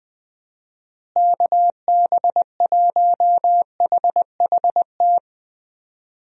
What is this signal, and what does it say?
Morse code, heard as a keyed tone, about 20 wpm, 700 Hz. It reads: KB1HHT
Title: USA Call Signs - 6 Characters - Repeated 3x - 20wpm